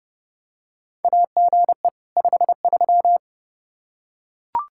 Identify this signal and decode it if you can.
Morse code, heard as a keyed tone, about 30 wpm, 700 Hz. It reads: AGE 53
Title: Practice QSO - Exchange 3 - More Info - 30wpm